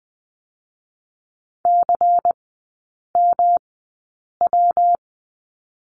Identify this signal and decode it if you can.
Morse code, heard as a keyed tone, about 20 wpm, 700 Hz. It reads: C M W
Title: 114 Lesson 19e - Letter w - Three Characters 20wpm